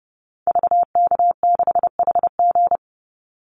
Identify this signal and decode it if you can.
Morse code, heard as a keyed tone, about 30 wpm, 700 Hz. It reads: VK6HG
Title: Call Signs - 30wpm